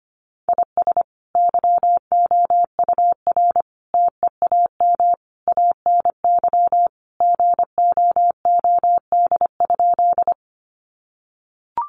Morse code, heard as a keyed tone, about 25 wpm, 700 Hz. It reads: IS YOUR TEAM ANY GOOD?
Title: Sentences from Top 1500 Words 25wpm